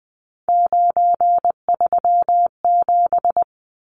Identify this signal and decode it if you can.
Morse code, heard as a keyed tone, about 20 wpm, 700 Hz. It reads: 937
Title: Numbers - 3-Digit 20wpm